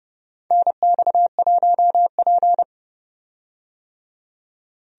Morse code, heard as a keyed tone, about 30 wpm, 700 Hz. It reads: NX1P
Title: USA Call Signs - 4 Characters - 30wpm